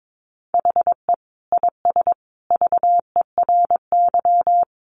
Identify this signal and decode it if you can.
Morse code, heard as a keyed tone, about 22 wpm, 700 Hz. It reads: HE IS VERY
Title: Sentences from Top 1400 Words 22wpm